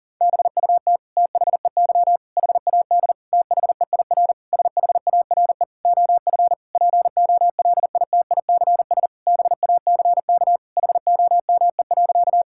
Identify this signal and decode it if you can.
Morse code, heard as a keyed tone, about 40 wpm, 700 Hz. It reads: BUT THEY HAD THEIR SHARE OF POLITICS BACK HOME.
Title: Sentences from Top 1000 Words - Review Part 6 - 40wpm